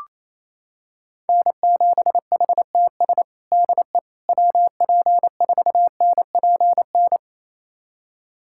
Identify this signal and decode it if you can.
Morse code, heard as a keyed tone, about 28 wpm, 700 Hz. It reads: N7HTS DE WP4NPN
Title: USA Callsign DE Callsign 28wpm